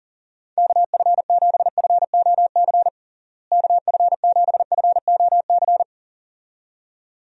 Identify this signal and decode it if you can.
Morse code, heard as a keyed tone, about 40 wpm, 700 Hz. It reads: KF7FOC KF7FOC
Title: USA Call Signs - 6 Characters - Repeated 2x - 40wpm